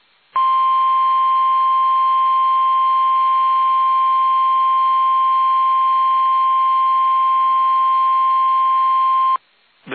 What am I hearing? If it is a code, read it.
Emergency Alert System (SAME) tones (the 1050 Hz attention tone). It is heard from 0.3 to 9.4 s.
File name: eas / KUSO / KUSO